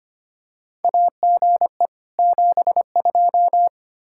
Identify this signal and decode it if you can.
Morse code, heard as a keyed tone, about 25 wpm, 700 Hz. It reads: AGE 72
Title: K1USN SST QSO Practice 25wpm